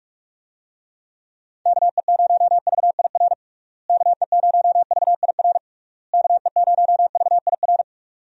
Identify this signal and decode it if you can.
Morse code, heard as a keyed tone, about 46 wpm, 700 Hz. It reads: KE0VIR KE0VIR KE0VIR
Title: USA Call Signs - 6 Characters - Repeated 3x - 45wpm